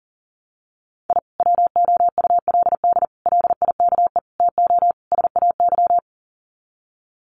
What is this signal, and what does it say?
Morse code, heard as a keyed tone, about 40 wpm, 700 Hz. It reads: I WOULD LIKE TO SAY
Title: Sets of 5 Words Encore2 40wpm